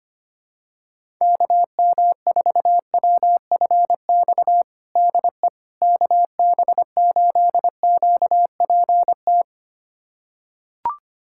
Morse code, heard as a keyed tone, about 25 wpm, 700 Hz. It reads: KM4WFX DE KB8QPT
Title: USA Callsign DE Callsign - 6 Characters - 25wpm